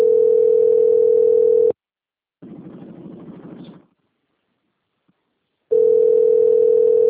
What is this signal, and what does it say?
Ringback tone (North America), 440 + 480 Hz.